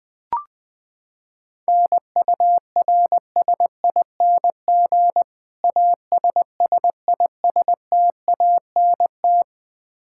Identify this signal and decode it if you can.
Morse code, heard as a keyed tone, about 20 wpm, 700 Hz. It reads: NURSING ASSISTANT
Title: Practice QSO - Exchange 3 - More Info - 20wpm